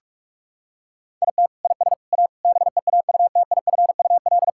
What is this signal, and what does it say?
Morse code, heard as a keyed tone, about 41 wpm, 700 Hz. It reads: IT IS A BEAUTIFUL
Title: Sentences from Top 1000 Words - Review Part 6 - 45wpm